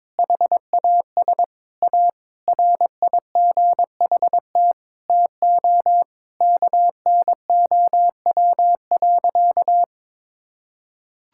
Morse code, heard as a keyed tone, about 22 wpm, 700 Hz. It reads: HAS A RIGHT TO KNOW.
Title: Sentences from Top 1000 Words - Review Part 6 - 22wpm